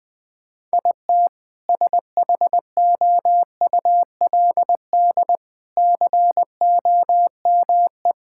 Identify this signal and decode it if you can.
Morse code, heard as a keyed tone, about 20 wpm, 700 Hz. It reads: IT SHOULD COME
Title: Sentences with 6-Word Phrases - Spoken Beginning - 20wpm